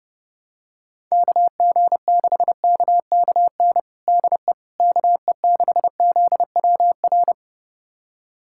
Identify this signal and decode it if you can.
Morse code, heard as a keyed tone, about 30 wpm, 700 Hz. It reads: KG6KKN DE KE6ZWR
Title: USA Callsign DE Callsign - 6 Characters - 30wpm